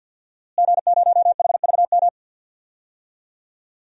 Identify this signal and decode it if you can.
Morse code, heard as a keyed tone, about 52 wpm, 700 Hz. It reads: G0HVM
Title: DXCC-100 Call Signs - 5 Characters - Repeated 3x - 50wpm